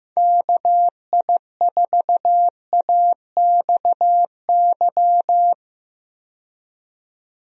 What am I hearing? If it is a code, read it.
Morse code, heard as a keyed tone, about 15 wpm, 700 Hz. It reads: KI4AXY
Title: USA Call Signs - 6 Characters - 15wpm